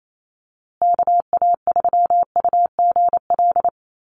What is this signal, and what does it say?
Morse code, heard as a keyed tone, about 28 wpm, 700 Hz. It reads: KA3UGL
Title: USA Call Signs Repeated 2x 28wpm